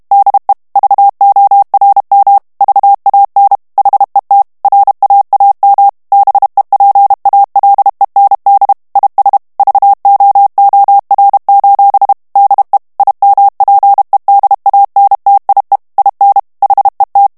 Morse code, heard as a keyed tone, about 32 wpm, 800 Hz. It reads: DE VORM VAN HET RAAM BEPALEND IS VOOR: DE IMPEDANTIE IN HET
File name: morse mp3 / HOLLANDAIS texte clair 1 vitesse 30 mots minute